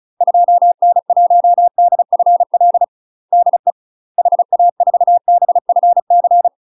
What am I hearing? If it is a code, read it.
Morse code, heard as a keyed tone, about 36 wpm, 700 Hz. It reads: 2N1DFL DE HA4BFC
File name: UK MASTER QSOs TRAINING / 35WPM / UK MASTER QSOs 0033